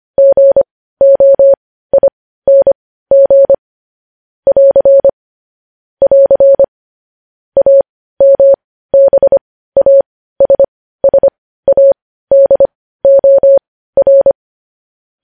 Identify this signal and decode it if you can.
Morse code, heard as a keyed tone, about 25 wpm, spaced as for 15 wpm, 550 Hz. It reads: GOING <AR> <AR> AMBASSADOR
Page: News Headlines in Morse Code at 15 WPM